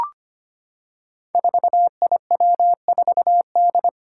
Morse code, heard as a keyed tone, about 25 wpm, 700 Hz. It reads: 4IW4D
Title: Five Challenging Letter-Number Combinations 25wpm